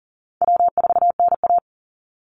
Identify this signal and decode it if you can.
Morse code, heard as a keyed tone, about 40 wpm, 700 Hz. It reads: W4NA
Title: Easy Call Signs Repeated 2x - 40wpm